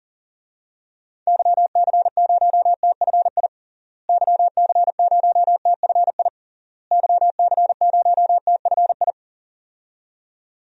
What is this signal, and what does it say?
Morse code, heard as a keyed tone, about 37 wpm, 700 Hz. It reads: YC0TFI YC0TFI YC0TFI
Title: DXCC-100 Call Signs - 6 Characters - Repeated 3x - 40wpm